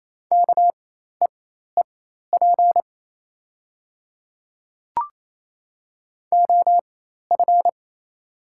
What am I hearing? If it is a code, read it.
Morse code, heard as a keyed tone, about 28 wpm, spaced as for 13 wpm, 700 Hz. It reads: KEEP OF
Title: 4x Character Spacing - Top 300 Words - 28wpm